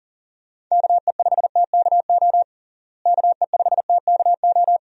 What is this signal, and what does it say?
Morse code, heard as a keyed tone, about 40 wpm, 700 Hz. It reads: KE5TKO KE5TKO
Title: USA Call Signs - 6 Characters - Repeated 2x - 40wpm